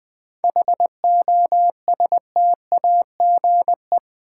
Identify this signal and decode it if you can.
Morse code, heard as a keyed tone, about 21 wpm, 700 Hz. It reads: HOSTAGE
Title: 7 Letter Words - First 3 Letters Given - 20wpm